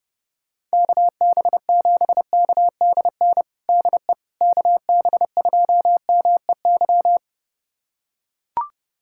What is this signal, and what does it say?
Morse code, heard as a keyed tone, about 30 wpm, 690 Hz. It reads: KB7KDN DE KB2MEY
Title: USA Callsign DE Callsign - 6 Characters - 30wpm